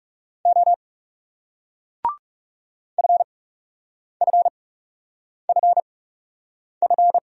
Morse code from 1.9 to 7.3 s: FFFF

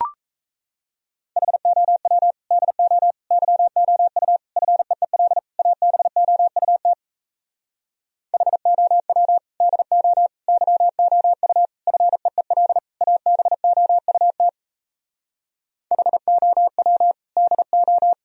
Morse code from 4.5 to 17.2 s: FEEL ABOUT HOW DO YOU FEEL ABOUT HOW

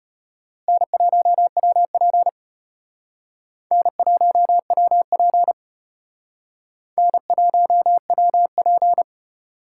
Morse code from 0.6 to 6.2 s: N1WP N1WP

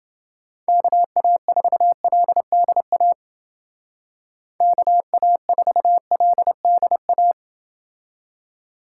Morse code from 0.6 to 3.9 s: KA4LDA

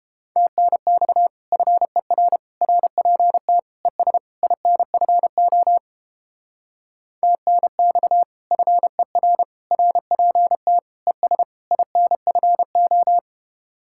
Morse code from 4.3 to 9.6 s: INFO TNX FER

